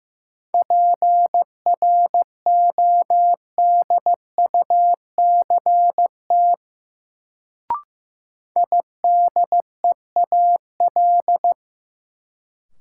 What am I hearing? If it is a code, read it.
Morse code, heard as a keyed tone, about 15 wpm, 700 Hz. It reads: PRODUCT IDEAL
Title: Sentences from Top 1400 Words - New Word Drill Alt - 15wpm